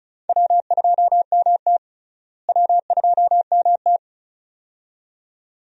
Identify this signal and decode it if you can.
Morse code, heard as a keyed tone, about 36 wpm, 700 Hz. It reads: W2MT W2MT
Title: USA Call Signs - 4 Characters - Repeated 2x - 35wpm